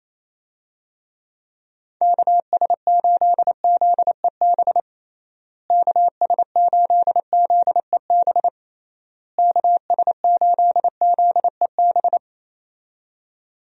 Morse code, heard as a keyed tone, about 28 wpm, 700 Hz. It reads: KS8ZEB KS8ZEB KS8ZEB
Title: USA Call Signs - 6 Characters - Repeated 3x - 28wpm